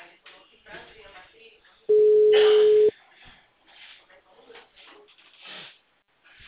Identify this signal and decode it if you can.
Ringback tone (Europe, 425 Hz), with speech.